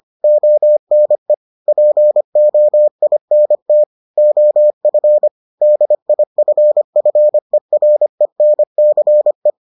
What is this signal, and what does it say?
Morse code, heard as a keyed tone, about 25 wpm, 600 Hz. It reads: ONE POINT OF DIFFERENCE